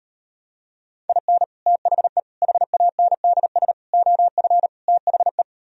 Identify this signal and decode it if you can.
Morse code, heard as a keyed tone, about 38 wpm, 700 Hz. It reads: IN THE HANDS OF THE